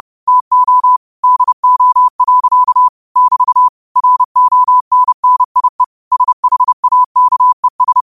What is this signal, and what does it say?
Morse code, heard as a keyed tone, about 30 wpm, 1000 Hz. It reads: TO DO. <BT> RONNIE SHAKES
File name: qotdfiles / 30wpm